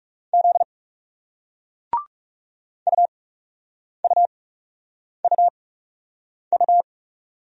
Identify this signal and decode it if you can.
Morse code, heard as a keyed tone, about 42 wpm, spaced as for 8 wpm, 700 Hz. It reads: Z UUUU